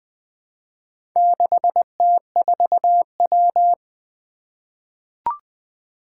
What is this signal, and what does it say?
Morse code, heard as a keyed tone, about 20 wpm, 700 Hz. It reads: 6T4W
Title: Four Challenging Letter-Number Combinations 20wpm